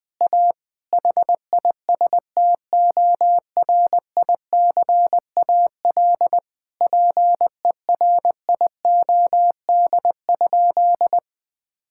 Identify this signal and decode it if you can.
Morse code, heard as a keyed tone, about 21 wpm, 700 Hz. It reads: A HISTORICAL PERIOD?
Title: Sentences from Top 1000 Words - Review Part 5 - 20wpm